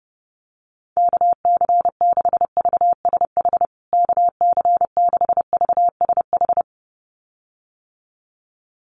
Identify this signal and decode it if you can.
Morse code, heard as a keyed tone, about 30 wpm, 700 Hz. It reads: KC6VSH KC6VSH
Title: USA Call Signs Repeated 2x 30wpm